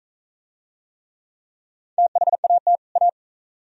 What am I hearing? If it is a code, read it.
Morse code, heard as a keyed tone, about 40 wpm, 700 Hz. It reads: THAT A